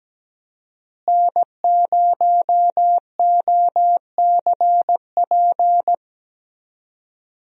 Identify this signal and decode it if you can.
Morse code, heard as a keyed tone, about 17 wpm, 700 Hz. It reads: N0OCP
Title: USA Call Signs - 5 Characters - Repeated 3x - 17wpm